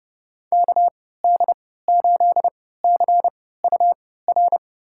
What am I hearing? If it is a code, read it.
Morse code, heard as a keyed tone, about 30 wpm, spaced as for 17 wpm, 700 Hz. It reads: KD8CUR
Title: USA Call Signs - 6 Characters - 3x Character Spacing - 30wpm